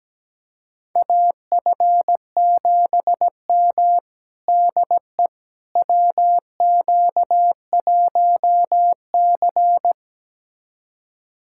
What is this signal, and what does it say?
Morse code, heard as a keyed tone, about 17 wpm, 700 Hz. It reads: AF7M DE WQ1C
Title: USA Callsign DE Callsign - 4 Characters - 17wpm